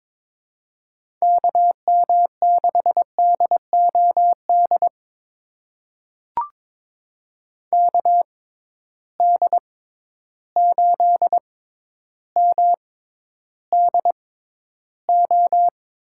Morse code, heard as a keyed tone, about 22 wpm, spaced as for 11 wpm, 700 Hz. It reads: KM6DOD KD8MDO
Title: USA Call Signs - 6 Characters - 6x Character Spacing - 22wpm